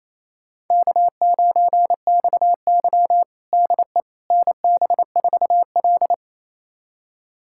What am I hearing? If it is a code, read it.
Morse code, heard as a keyed tone, about 28 wpm, 700 Hz. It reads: K9XY DE NB4L
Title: USA Callsign DE Callsign - 4 Characters - 28wpm